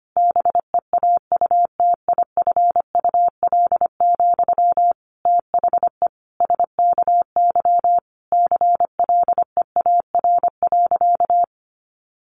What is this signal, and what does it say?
Morse code, heard as a keyed tone, about 25 wpm, 700 Hz. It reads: BEAUTIFUL, THE SKY CLEAR.
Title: Sentences from Top 1200 Words 25wpm